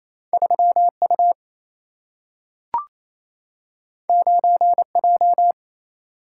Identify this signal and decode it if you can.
Morse code, heard as a keyed tone, about 28 wpm, 700 Hz. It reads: 3 U 9 J
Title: Single-Letter Single-Number 28wpm